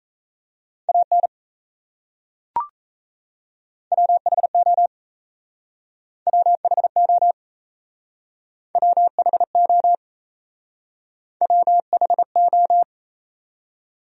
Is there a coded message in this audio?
Yes: Morse code from 0.9 to 12.8 s, heard as a keyed tone, about 37 wpm, 700 Hz. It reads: AN WHO WHO WHO WHO